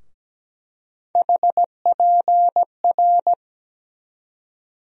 Morse code, heard as a keyed tone, about 17 wpm, 700 Hz. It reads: HPR
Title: USA Call Sign Postfixes - 3 Letters 17wpm